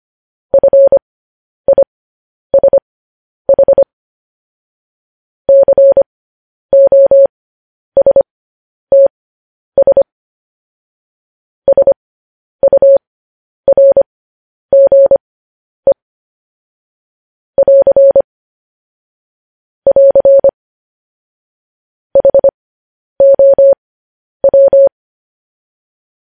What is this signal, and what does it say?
Morse code, heard as a keyed tone, about 25 wpm, spaced as for 10 wpm, 550 Hz. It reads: FISH COSTS SURGE <AR> <AR> HOW
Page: news headlines in morse code at 10 wpm